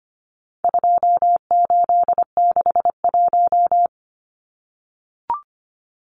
Morse code, heard as a keyed tone, about 25 wpm, 700 Hz. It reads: 2861
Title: Numbers - 4-Digit 25wpm